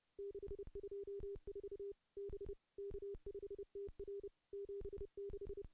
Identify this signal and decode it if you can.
Morse code, heard as a keyed tone, about 30 wpm, 400 Hz. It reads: 624 B K5TR 76
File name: contests / CW-SS-k5tr- -0400